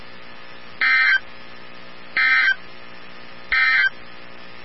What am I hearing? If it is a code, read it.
Emergency Alert System (SAME) tones, heard as harsh buzzing bursts. It is heard from 0.7 to 4.0 s.
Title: NOAA Weather Radio 162.550